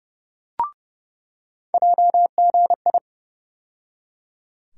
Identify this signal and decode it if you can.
Morse code, heard as a keyed tone, about 30 wpm, 700 Hz. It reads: JGI
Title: USA Call Sign Postfixes - 3 Letters 30wpm